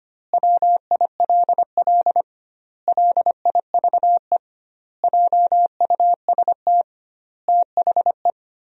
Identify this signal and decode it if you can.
Morse code, heard as a keyed tone, about 25 wpm, 700 Hz. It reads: WILL LIVE JUST THE
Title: 2x Word-Spacing - Sentences from Top 300 Words - 25wpm